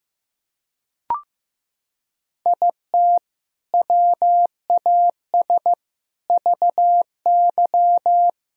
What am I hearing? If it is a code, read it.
Morse code, heard as a keyed tone, about 15 wpm, 700 Hz. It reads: IT WAS VY
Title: Sentences from Top 300 Words and Abbreviations 15wpm